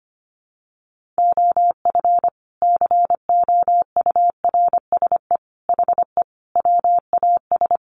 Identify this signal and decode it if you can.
Morse code, heard as a keyed tone, about 25 wpm, 700 Hz. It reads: OF COURSE HE WAS
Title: Sentences from Top 500 Words 25wpm